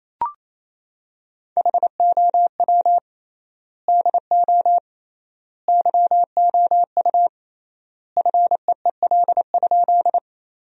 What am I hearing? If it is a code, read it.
Morse code, heard as a keyed tone, about 28 wpm, 700 Hz. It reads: HOW DO YOU FEEL?
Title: 3x Word-Spacing - Sentences from Top 400 Words - 28wpm